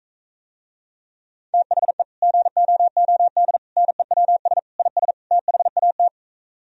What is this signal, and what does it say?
Morse code, heard as a keyed tone, about 44 wpm, 700 Hz. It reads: THE GOOD NEWS IS THAT